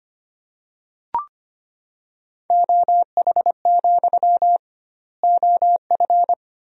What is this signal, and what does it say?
Morse code, heard as a keyed tone, about 25 wpm, 700 Hz. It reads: OH, OF
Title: 2x Word-Spacing - Sentences from Top 500 Words - 25wpm